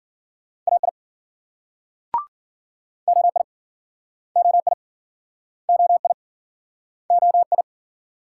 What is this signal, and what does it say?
Morse code, heard as a keyed tone, about 56 wpm, 700 Hz. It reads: AI OI OI OI OI